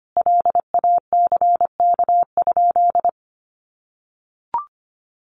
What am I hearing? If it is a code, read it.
Morse code, heard as a keyed tone, about 25 wpm, 700 Hz. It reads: LACK?
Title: Sentences from Top 1100 Words 25wpm